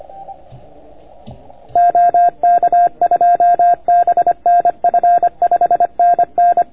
Morse code, heard as a keyed tone, about 27 wpm, 690 Hz. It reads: OK2BNF 5NN